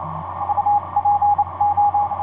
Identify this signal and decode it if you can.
Morse code, heard as a keyed tone, about 28 wpm, spaced as for 24 wpm, 820 Hz, with radio noise. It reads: UPO
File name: vp8dmn i1up etc 40m cw 28jan2010 2343